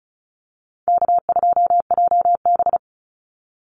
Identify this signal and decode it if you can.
Morse code, heard as a keyed tone, about 34 wpm, 700 Hz. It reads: K2JB
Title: NA SOTA Chaser Callsigns 35wpm